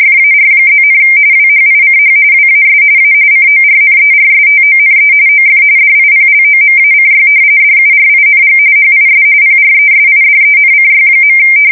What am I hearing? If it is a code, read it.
RTTY, about 45.45 baud. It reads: FAR+, WHICH WAS A CORNERSTONE OF DIPLOMACY DURING HIS PRESIDENCY.
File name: fsk170 / usnavy3